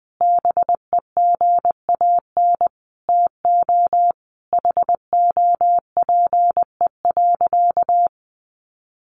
Morse code, heard as a keyed tone, about 19 wpm, 700 Hz. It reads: BEGAN TO HOPE.
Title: Sentences from Top 1100 Words 20wpm